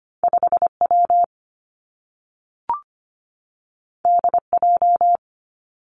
Morse code, heard as a keyed tone, about 25 wpm, 700 Hz. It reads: 5 W D J
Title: Two Challenging Letter-Number Combinations 25wpm